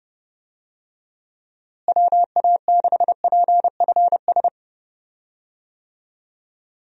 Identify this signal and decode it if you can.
Morse code, heard as a keyed tone, about 30 wpm, 700 Hz. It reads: WA6PFS
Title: USA Call Signs - 6 Characters - Repeated 2x - 30wpm